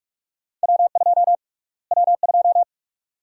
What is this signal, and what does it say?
Morse code, heard as a keyed tone, about 46 wpm, 700 Hz. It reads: W2 W2
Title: USA Call Sign Prefixes - Repeated 2x - 45wpm